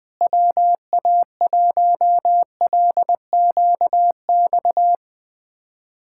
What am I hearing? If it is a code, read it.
Morse code, heard as a keyed tone, about 20 wpm, 700 Hz. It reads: WA1LQX
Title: USA Callsign DE Callsign - 6 Characters - 20wpm